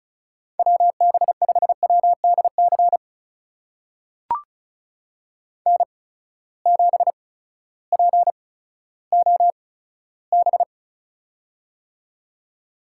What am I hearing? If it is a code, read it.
Morse code, heard as a keyed tone, about 36 wpm, spaced as for 16 wpm, 700 Hz. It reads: WB5WDC N7POB
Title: USA Call Signs - 8x Character Spacing - 35wpm